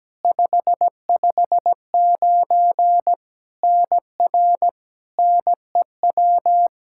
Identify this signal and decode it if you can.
Morse code, heard as a keyed tone, about 17 wpm, 700 Hz. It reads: 559 NR NEW
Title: K1USN SST QSO Practice 17wpm